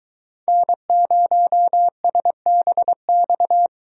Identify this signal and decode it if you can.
Morse code, heard as a keyed tone, about 22 wpm, 700 Hz. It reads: N0SBX